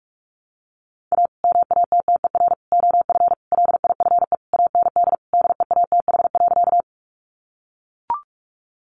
Morse code, heard as a keyed tone, about 44 wpm, 700 Hz. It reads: A MATTER OF LIFE AND DEATH.
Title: Sentences from Top 1100 Words 45wpm